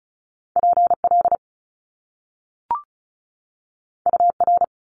Morse code, heard as a keyed tone, about 36 wpm, 700 Hz. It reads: PL UR
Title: Top Two Letter Combinations in Words 35wpm